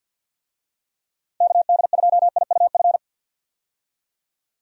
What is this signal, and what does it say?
Morse code, heard as a keyed tone, about 51 wpm, 700 Hz. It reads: KD2IUF
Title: USA Call Signs - 6 Characters - Repeated 2x - 50wpm